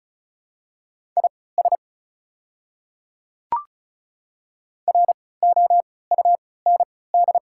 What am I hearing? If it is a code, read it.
Morse code, heard as a keyed tone, about 34 wpm, spaced as for 20 wpm, 690 Hz. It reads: IS ROUND
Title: 3x Character Spacing - Top 300 Words - 35wpm